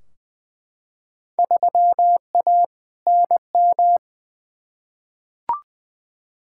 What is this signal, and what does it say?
Morse code, heard as a keyed tone, about 20 wpm, 700 Hz. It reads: 3A NM
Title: Field Day Exchanges Encore - 20wpm